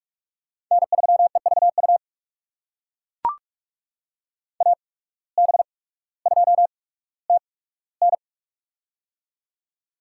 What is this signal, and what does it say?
Morse code, heard as a keyed tone, about 42 wpm, spaced as for 19 wpm, 700 Hz. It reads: N3EVU AB2TN
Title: USA Call Signs - 5 Characters - 8x Character Spacing - 45wpm